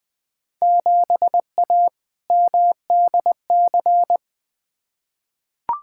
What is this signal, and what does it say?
Morse code, heard as a keyed tone, about 20 wpm, 700 Hz. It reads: 7A MDC
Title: Field Day Exchanges Encore - 20wpm